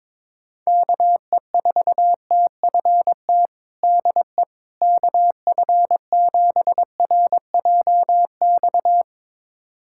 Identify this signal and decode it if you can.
Morse code, heard as a keyed tone, about 21 wpm, 700 Hz. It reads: KE4TFT DE KF7RJX
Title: USA Callsign DE Callsign - 6 Characters - 22wpm